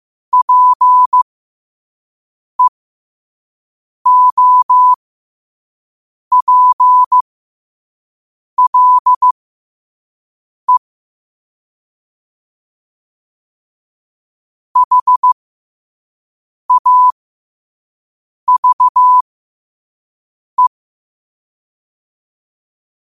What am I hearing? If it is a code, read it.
Morse code, heard as a keyed tone, about 15 wpm, spaced as for 5 wpm, 1000 Hz. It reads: PEOPLE HAVE